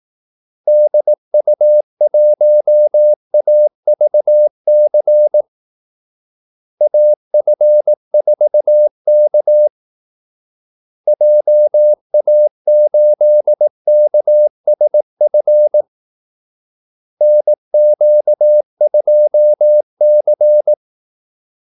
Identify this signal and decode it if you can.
Morse code, heard as a keyed tone, about 18 wpm, 600 Hz. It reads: DU1AVC AF4K JA8KSF NQ2C
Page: Kaldesignaler 18 wpm